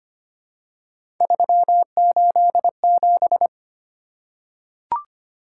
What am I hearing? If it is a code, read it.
Morse code, heard as a keyed tone, about 25 wpm, 700 Hz. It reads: 387
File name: Three Letter-Number Combinations - 25wpm